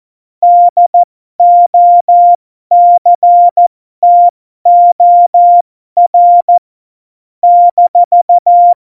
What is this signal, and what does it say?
Morse code, heard as a keyed tone, about 14 wpm, spaced as for 12 wpm, 700 Hz. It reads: DOCTOR -
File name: Morsekurs / Witze